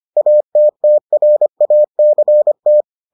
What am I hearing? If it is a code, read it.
Morse code, heard as a keyed tone, about 25 wpm, 600 Hz. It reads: ATTRACT